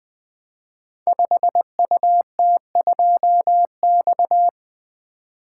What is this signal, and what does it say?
Morse code, heard as a keyed tone, about 20 wpm, 700 Hz. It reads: 5UT2X
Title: Code Groups - 5 Letter:Numbers - 20wpm